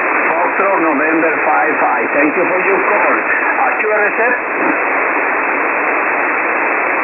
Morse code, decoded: none